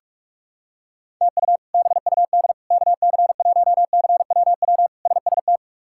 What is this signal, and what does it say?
Morse code, heard as a keyed tone, about 46 wpm, 700 Hz. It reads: TU BUD KC1CWW SST
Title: K1USN SST QSO Practice - 45wpm